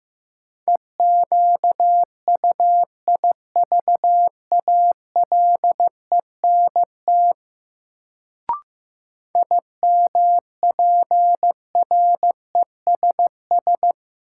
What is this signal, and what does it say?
Morse code, heard as a keyed tone, about 16 wpm, 700 Hz. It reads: EQUIVALENT IMPRESS
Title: Sentences from Top 1600 Words - New Word Drill Alt - 15wpm